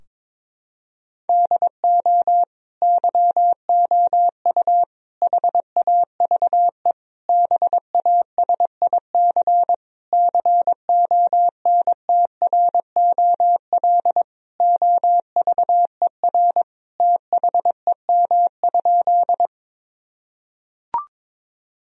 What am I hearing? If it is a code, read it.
Morse code, heard as a keyed tone, about 21 wpm, 700 Hz. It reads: DO YOU HAVE BASIC CONTROL OVER THEM?
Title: Sentences from Top 1000 Words - Review Part 3 - 22wpm